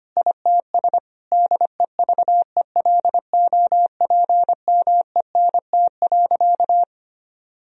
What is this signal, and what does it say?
Morse code, heard as a keyed tone, about 25 wpm, 700 Hz. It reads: ITS DEVELOPMENT.
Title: Sentences from Top 1000 Words - Review Part 4 - 25wpm